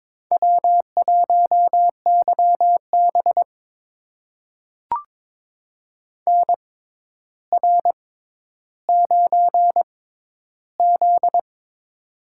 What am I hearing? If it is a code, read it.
Morse code, heard as a keyed tone, about 22 wpm, spaced as for 11 wpm, 700 Hz. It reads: W1YB NR9Z
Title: USA Call Signs - 4 Characters - 6x Character Spacing - 22wpm